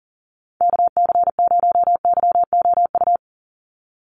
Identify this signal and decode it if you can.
Morse code, heard as a keyed tone, about 40 wpm, 700 Hz. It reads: KC0YOU
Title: USA Call Signs 40wpm